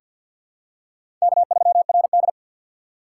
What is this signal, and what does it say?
Morse code, heard as a keyed tone, about 48 wpm, 700 Hz. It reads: K3RD
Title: USA Call Signs - 4 Characters - 6x Character Spacing - 50wpm